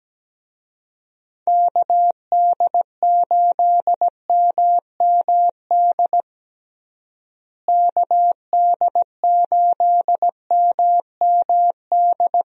Morse code, heard as a keyed tone, about 17 wpm, 700 Hz. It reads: KD8MMD KD8MMD
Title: USA Call Signs - 6 Characters - Repeated 3x - 17wpm